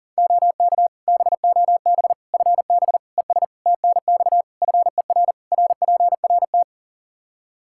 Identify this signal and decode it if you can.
Morse code, heard as a keyed tone, about 40 wpm, 700 Hz. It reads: OK BOB FB ES TNX FER RPRT
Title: Practice QSO - Exchange 2 - Info - 40wpm